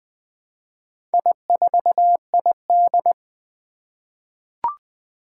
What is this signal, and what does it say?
Morse code, heard as a keyed tone, about 20 wpm, 700 Hz. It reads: I4ID
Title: Four Challenging Letter-Number Combinations 20wpm